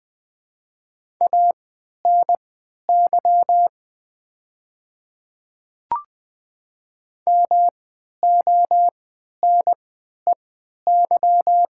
Morse code, heard as a keyed tone, about 20 wpm, spaced as for 11 wpm, 700 Hz. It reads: ANY MONEY
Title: 3x Character Spacing - Top 500 Words - 20wpm